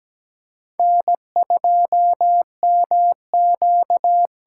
Morse code, heard as a keyed tone, about 17 wpm, 700 Hz. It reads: N2MQ
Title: USA Call Signs - 5 Characters - 8x Character Spacing - 17wpm